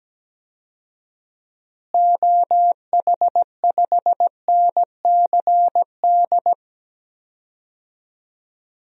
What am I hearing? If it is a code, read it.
Morse code, heard as a keyed tone, about 17 wpm, 700 Hz. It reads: OH5NCD
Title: DXCC-100 Call Signs - 6 Characters - Repeated 3x - 17wpm